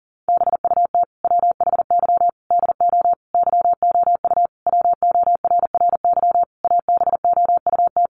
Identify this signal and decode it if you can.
Morse code, heard as a keyed tone, about 40 wpm, 700 Hz. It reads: BUT WHY DO YOU WORRY ABOUT
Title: Sentences from Top 1100 Words 40wpm